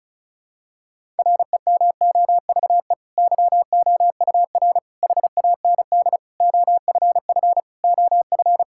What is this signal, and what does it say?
Morse code, heard as a keyed tone, about 36 wpm, 700 Hz. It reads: REMOVE YOUR HAND OFF OF
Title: Sentences from Top 1000 Words - Review Part 5 - 35wpm